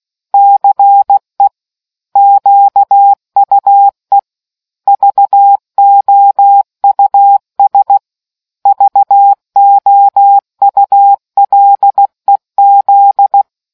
Morse code, heard as a keyed tone, about 16 wpm, 800 Hz. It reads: CE QUE VOUS VOULEZ